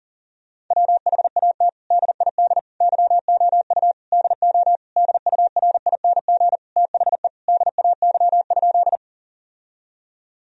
Morse code, heard as a keyed tone, about 40 wpm, 700 Hz. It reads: WHAT DID YOU DO DURING THE DAY?
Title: Sentences from Top 1000 Words - Review Part 2 - 40wpm